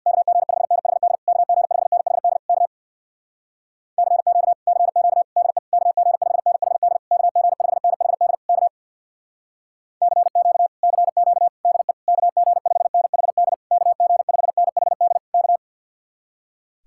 Morse code, heard as a keyed tone, about 65 wpm, 700 Hz. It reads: KG5NHD KG5NHD K CQ CQ DE KG5NHD KG5NHD K CQ CQ DE KG5NHD KG5NHD K